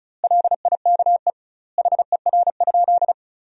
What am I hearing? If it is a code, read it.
Morse code, heard as a keyed tone, about 36 wpm, 700 Hz. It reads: LIKE HER?
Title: 2x Word-Spacing - Sentences from Top 100 Words - 35wpm